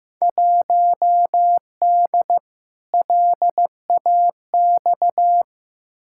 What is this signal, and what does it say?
Morse code, heard as a keyed tone, about 15 wpm, 700 Hz. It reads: 1D LAX
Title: Field Day Exchanges - 15wpm